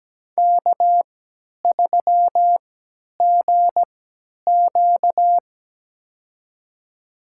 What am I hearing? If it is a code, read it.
Morse code, heard as a keyed tone, about 17 wpm, spaced as for 10 wpm, 700 Hz. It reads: K3GQ
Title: USA Call Signs - 4 Characters - 3x Character Spacing - 17wpm